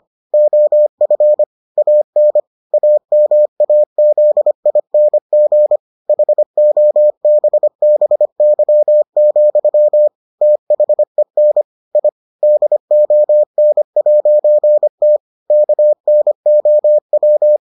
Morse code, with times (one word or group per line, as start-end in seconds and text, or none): 0.3-1.4 OF
1.8-2.4 AN
2.7-5.8 AMAZING
6.1-10.1 HOBBY,
10.4-11.6 THEN
11.9-12.1 I
12.4-15.2 DON'T
15.5-17.6 KNOW